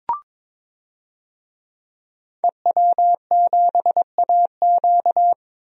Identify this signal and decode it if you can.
Morse code, heard as a keyed tone, about 22 wpm, 700 Hz. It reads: EW7AQ
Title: DXCC-100 Call Signs - 5 Characters - 22wpm